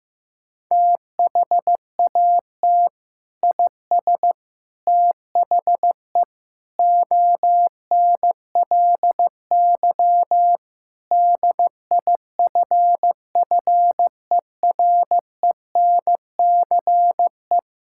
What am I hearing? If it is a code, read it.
Morse code, heard as a keyed tone, about 15 wpm, 700 Hz. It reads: THAT IS THE ONLY DIFFERENCE
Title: Sentences from Top 1000 Words - Review Part 5 - 15wpm